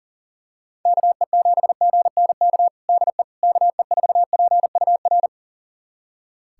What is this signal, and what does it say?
Morse code, heard as a keyed tone, about 40 wpm, 700 Hz. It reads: KE7GNK DE KE4PUR
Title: USA Callsign DE Callsign - 6 Characters - 40wpm